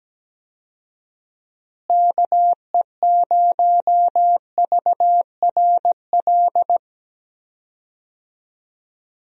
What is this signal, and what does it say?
Morse code, heard as a keyed tone, about 17 wpm, 700 Hz. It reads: KE0VRL
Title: USA Call Signs - 6 Characters - Repeated 3x - 17wpm